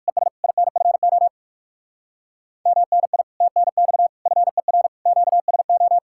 Morse code, heard as a keyed tone, about 56 wpm, 700 Hz. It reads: ES INFO MNI TNX FER QSO
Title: Speed-Racing Common QSO Phrases - 45wpm